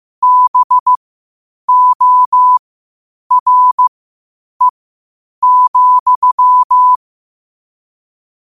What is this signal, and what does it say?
Morse code, heard as a keyed tone, about 15 wpm, spaced as for 8 wpm, 1000 Hz. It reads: BORE,
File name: qotdfiles / 8wpm